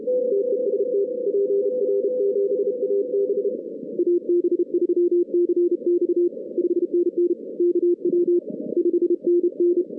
Morse code, heard as a keyed tone, about 32 wpm, 340 Hz. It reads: TEST PR7AB AB3CX 5NN KW 5NN
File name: 2016 arrl dx cw / 2016-ARRL-DX-CW-K5TR- - 0-41